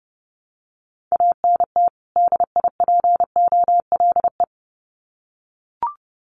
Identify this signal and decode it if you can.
Morse code, heard as a keyed tone, about 30 wpm, 700 Hz. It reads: ANT DIPOLE
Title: Common QSO Phrases 30wpm